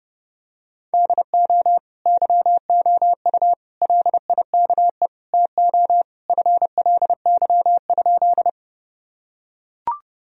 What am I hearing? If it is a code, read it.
Morse code, heard as a keyed tone, about 32 wpm, 700 Hz. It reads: DO YOU LIKE TO FLY?
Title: Sentences from Top 1000 Words - Review Part 2 - 30wpm